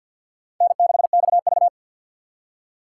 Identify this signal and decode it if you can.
Morse code, heard as a keyed tone, about 62 wpm, 700 Hz. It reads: N6XV
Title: Speed Racing - USA 1x2 Call Signs 50wpm